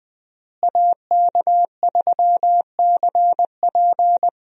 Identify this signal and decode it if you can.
Morse code, heard as a keyed tone, about 21 wpm, 700 Hz. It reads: AK3CP
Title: USA Call Signs - 5 Characters - Repeated 2x - 20wpm